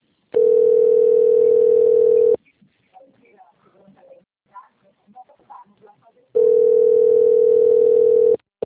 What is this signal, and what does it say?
Ringback tone (North America), 440 + 480 Hz, with speech.
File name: rg-1600-unknown- - - .35380